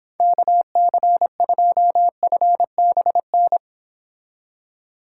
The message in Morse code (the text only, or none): KC2FBN